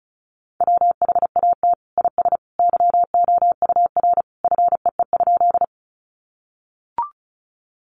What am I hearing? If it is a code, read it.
Morse code, heard as a keyed tone, about 34 wpm, 700 Hz. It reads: WHAT IS YOUR FEE?
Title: Sentences from Top 1500 Words 35wpm